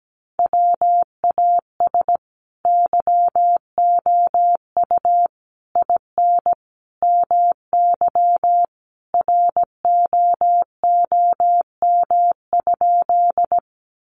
Morse code, heard as a keyed tone, about 17 wpm, 700 Hz. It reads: WAS YOU IN MY ROOM?
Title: Sentences from Top 300 Words 17wpm